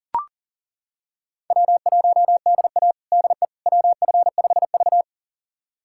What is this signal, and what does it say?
Morse code, heard as a keyed tone, about 40 wpm, 700 Hz. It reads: W1DA DE WF5V
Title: USA Callsign DE Callsign - 4 Characters - 40wpm